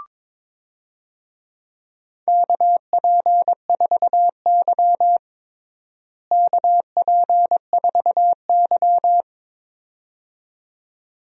Morse code, heard as a keyed tone, about 22 wpm, 700 Hz. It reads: KP4Y KP4Y
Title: DXCC-100 Call Signs - 4 Characters - Repeated 2x - 22wpm